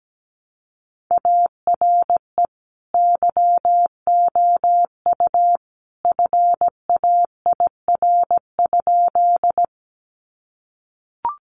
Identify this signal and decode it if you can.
Morse code, heard as a keyed tone, about 17 wpm, 700 Hz. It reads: ARE YOU FAIR?
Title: Sentences from Top 1300 Words 17wpm